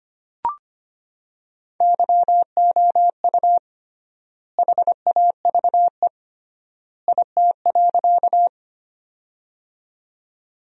Morse code, heard as a keyed tone, about 25 wpm, 700 Hz. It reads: YOU HAVE IT.
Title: 3x Word-Spacing - Sentences from Top 100 Words - 25wpm